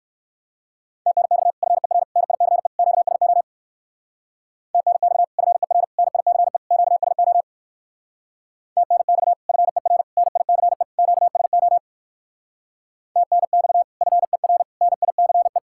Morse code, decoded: TNX FER NICE QSO TNX FER NICE QSO TNX FER NICE QSO TNX FER NICE

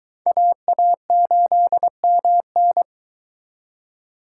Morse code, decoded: AA8MN

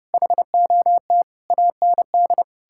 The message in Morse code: HOT AND

